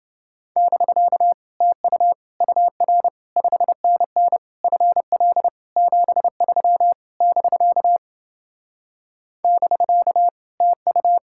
Morse code, heard as a keyed tone, about 29 wpm, 700 Hz. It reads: <BK> TU UR 5NN FL 73 <BK> <BK> TU